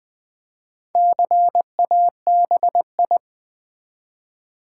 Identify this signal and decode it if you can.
Morse code, heard as a keyed tone, about 20 wpm, 700 Hz. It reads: CABI
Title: Build A Word - 1 Letter At A Time - 5-Letter Words - 20wpm